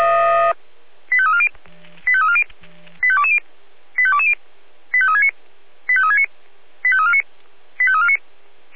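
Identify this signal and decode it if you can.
Five-tone selective call (ZVEI), heard as quick runs of five tones, tones of about 70 ms. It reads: 85340 85340 85220 85220 85369 85369 85349 85349
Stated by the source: Five-tone ZVEI selective call received on SDR.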